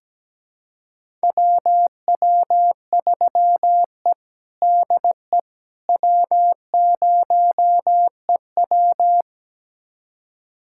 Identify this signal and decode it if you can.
Morse code, heard as a keyed tone, about 17 wpm, 700 Hz. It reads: WW3E DE W0EW
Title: USA Callsign DE Callsign - 4 Characters - 17wpm